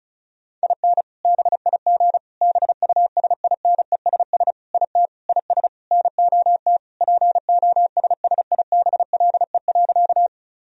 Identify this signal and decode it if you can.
Morse code, heard as a keyed tone, about 36 wpm, 700 Hz. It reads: IN BIG BUSINESS IT IS NOT POSSIBLE.
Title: Sentences from Top 1000 Words - Review Part 1 - 35wpm